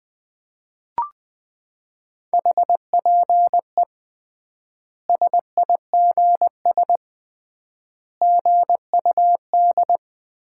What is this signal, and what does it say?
Morse code, heard as a keyed tone, about 20 wpm, 700 Hz. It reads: HPE SIGS GUD
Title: Common QSO Phrases - 3x Word-Spacing - 20wpm